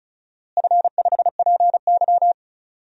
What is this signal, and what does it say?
Morse code, heard as a keyed tone, about 36 wpm, 700 Hz. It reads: F5PY
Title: DXCC-100 Call Signs - 4 Characters - Repeated 2x - 35wpm